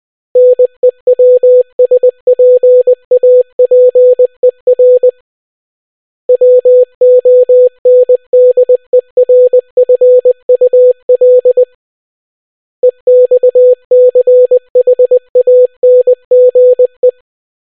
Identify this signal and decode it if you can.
Morse code, heard as a keyed tone, about 20 wpm, 500 Hz. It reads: NEWSPAPER WONDERFUL EXCHANGE